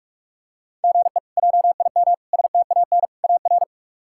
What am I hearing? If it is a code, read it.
Morse code, heard as a keyed tone, about 46 wpm, 700 Hz. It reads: GE JIM STAN AR
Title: K1USN SST QSO Practice - 45wpm